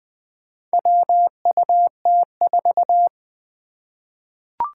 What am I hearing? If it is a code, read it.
Morse code, heard as a keyed tone, about 20 wpm, 700 Hz. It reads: WUT4
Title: Four Challenging Letter-Number Combinations 20wpm